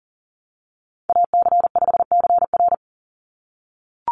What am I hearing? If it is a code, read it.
Morse code, heard as a keyed tone, about 40 wpm, 700 Hz. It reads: AC5CR
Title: USA Call Signs 40wpm